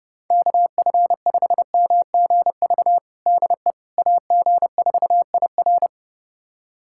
Morse code, heard as a keyed tone, about 30 wpm, 700 Hz. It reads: KF5MGV DE AG4IR
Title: USA Callsign DE Callsign 30wpm